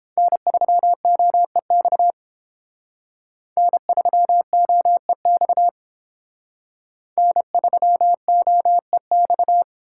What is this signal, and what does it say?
Morse code, heard as a keyed tone, about 30 wpm, 700 Hz. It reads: N3OEX N3OEX N3OEX